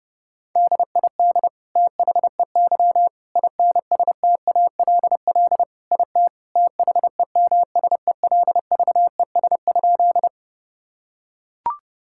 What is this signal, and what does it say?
Morse code, heard as a keyed tone, about 30 wpm, 700 Hz. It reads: DID THEY INSTALL IT THEMSELVES?
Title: Sentences from Top 1600 Words 30wpm